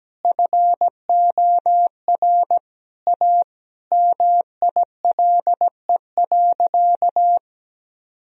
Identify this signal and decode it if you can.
Morse code, heard as a keyed tone, about 17 wpm, 700 Hz. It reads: FOR A MILE.
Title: Sentences from Top 1000 Words - Review Part 4 - 17wpm